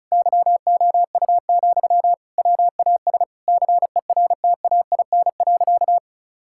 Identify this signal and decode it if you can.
Morse code, heard as a keyed tone, about 36 wpm, 700 Hz. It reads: YOU, WAS CERTAIN.
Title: Sentences from Top 1000 Words - Review Part 6 - 35wpm